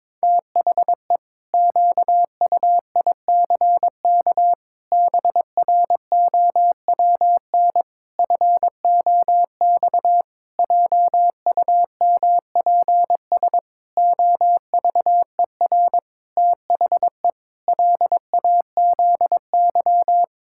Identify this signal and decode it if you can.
Morse code, heard as a keyed tone, about 22 wpm, 700 Hz. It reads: THE QUICK BROWN FOX JUMPS OVER THE LAZY
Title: Pangram Challenge v2 - 22wpm